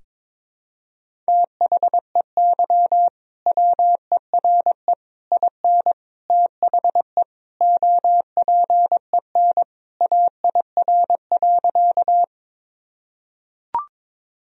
Morse code, heard as a keyed tone, about 22 wpm, 700 Hz. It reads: THEY WERE IN THE OPEN AIR.
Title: Sentences from Top 1000 Words - Review Part 1 - 22wpm